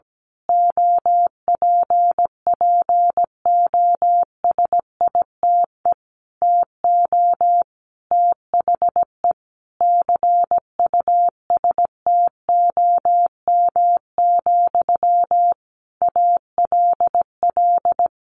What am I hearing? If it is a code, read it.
Morse code, heard as a keyed tone, about 17 wpm, 700 Hz. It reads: OPPOSITE TO THE CUSTOM, ALL
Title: A Sound Of Thunder 17wpm